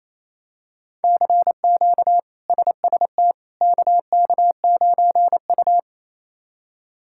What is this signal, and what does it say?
Morse code, heard as a keyed tone, about 28 wpm, 700 Hz. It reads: CQ SST KK9U
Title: K1USN SST QSO Practice - 28wpm